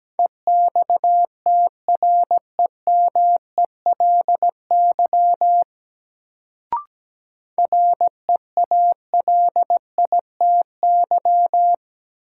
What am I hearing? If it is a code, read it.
Morse code, heard as a keyed tone, about 17 wpm, 700 Hz. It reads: EXTREMELY REALITY
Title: Sentences from Top 1200 Words - New Word Drill Alt - 17wpm